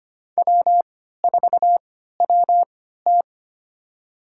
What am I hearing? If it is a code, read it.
Morse code, heard as a keyed tone, about 25 wpm, spaced as for 14 wpm, 700 Hz. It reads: W4WT
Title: USA Call Signs - 4 Characters - 3x Character Spacing - 25wpm